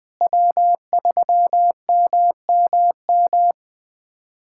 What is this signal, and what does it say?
Morse code, heard as a keyed tone, about 21 wpm, 700 Hz. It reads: W3MMM
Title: K1USN SST QSO Practice - 20wpm